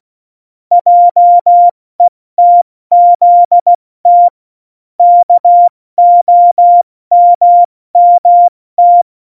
Morse code, heard as a keyed tone, about 16 wpm, 700 Hz. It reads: JETZT KOMMT
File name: Morsekurs / Witze / Witz 17. Oktober 17, Tempo 70